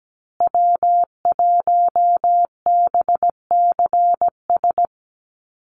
Morse code, heard as a keyed tone, about 17 wpm, 700 Hz. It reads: W1BCS
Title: USA Call Signs Repeated 3x 17wpm